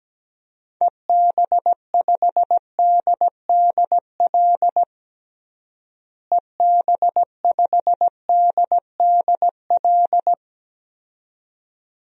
Morse code, heard as a keyed tone, about 17 wpm, 700 Hz. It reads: EB5DDL EB5DDL
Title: DXCC-100 Call Signs - 6 Characters - Repeated 2x - 17wpm